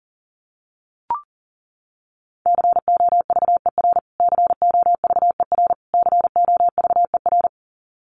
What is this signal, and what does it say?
Morse code, heard as a keyed tone, about 40 wpm, 700 Hz. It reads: COVER COVER COVER
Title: Top 500 Words Repeated 40wpm